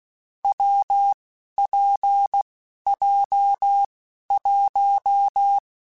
Morse code, heard as a keyed tone, about 16 wpm, spaced as for 11 wpm, 800 Hz. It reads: WPJ1
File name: Difficult Combos / Difficult Combos-mp3s / WPJ WPJ1 @10